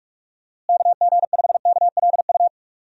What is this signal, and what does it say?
Morse code, heard as a keyed tone, about 47 wpm, 700 Hz. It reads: KG5KLU
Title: USA Call Signs - 6 Characters - Repeated 2x - 45wpm